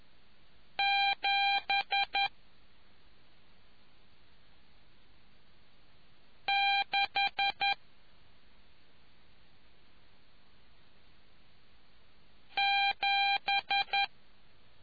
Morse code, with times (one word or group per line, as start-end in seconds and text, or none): 0.8-2.3 7
6.5-7.7 6
12.6-14.0 7